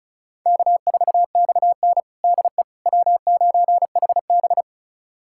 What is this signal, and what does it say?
Morse code, heard as a keyed tone, about 36 wpm, 700 Hz. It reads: K4XN DE W9HB
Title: USA Callsign DE Callsign - 4 Characters - 35wpm